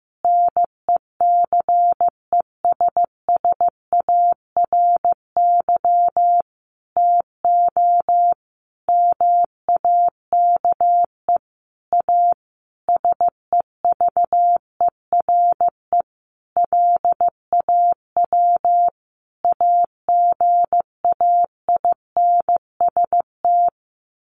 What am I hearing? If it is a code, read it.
Morse code, heard as a keyed tone, about 15 wpm, 700 Hz. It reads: NECESSARY TO MAKE A SEVERE LAW AGAINST
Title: Sentences from Top 1400 Words 15wpm